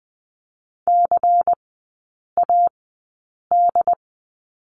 Morse code, heard as a keyed tone, about 20 wpm, spaced as for 8 wpm, 700 Hz. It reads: CAD
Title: 093 Lesson 16e - Letter c - Three Characters 20wpm